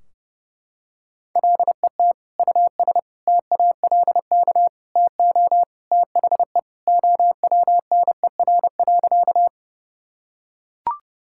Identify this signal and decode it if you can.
Morse code, heard as a keyed tone, about 32 wpm, 700 Hz. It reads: LET US TALK TO THE OWNER.
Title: Sentences from Top 1600 Words 30wpm